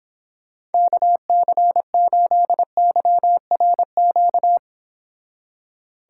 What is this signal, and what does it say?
Morse code, heard as a keyed tone, about 31 wpm, 700 Hz. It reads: KC8YRQ